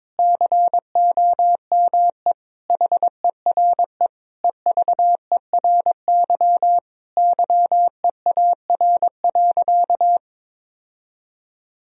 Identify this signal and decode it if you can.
Morse code, heard as a keyed tone, about 22 wpm, 700 Hz. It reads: COME HERE EVERY YEAR.
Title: Sentences from Top 1000 Words - Review Part 1 - 22wpm